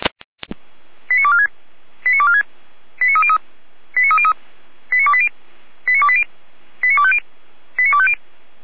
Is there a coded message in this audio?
Yes: five-tone selective call (ZVEI) from 1.1 to 8.1 s, heard as quick runs of five tones, tones of about 70 ms. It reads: 89246 89246 89303 89303 89280 89280 89250 89250